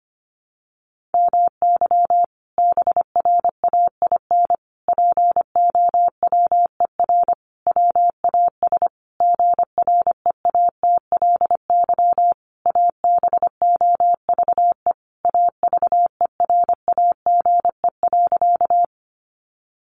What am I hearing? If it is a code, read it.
Morse code, heard as a keyed tone, about 25 wpm, 700 Hz. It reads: MY BRAIN POWER WAS GREATLY ABOVE AVERAGE.
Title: Sentences from Top 1400 Words 25wpm